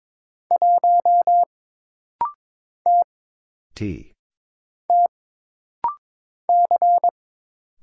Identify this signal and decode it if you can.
Morse code, heard as a keyed tone, about 22 wpm, 700 Hz. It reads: 1 T T C
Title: Single Letter-Number Alt - Rapid-Fire 22wpm